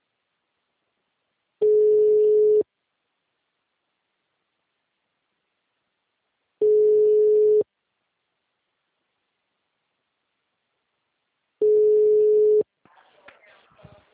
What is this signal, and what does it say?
Ringback tone (Europe, 425 Hz).